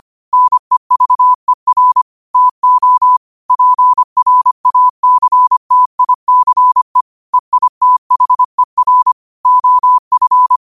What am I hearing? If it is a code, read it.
Morse code, heard as a keyed tone, about 25 wpm, 1000 Hz. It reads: NEVER TO PRACTICE EITHER OF
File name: qotdfiles / 25wpm